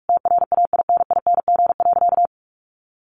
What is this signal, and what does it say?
Morse code, heard as a keyed tone, about 44 wpm, 700 Hz. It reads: TRAINING.
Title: Sentences from Top 1400 Words 45wpm